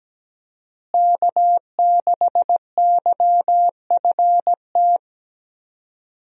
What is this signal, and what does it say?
Morse code, heard as a keyed tone, about 17 wpm, 700 Hz. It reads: K6YFT
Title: USA Call Signs - 5 Characters - Repeated 3x - 17wpm